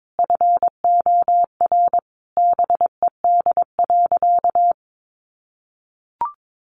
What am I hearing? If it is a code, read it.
Morse code, heard as a keyed tone, about 22 wpm, 700 Hz. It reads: FOR BED.
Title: Sentences from Top 500 Words 22wpm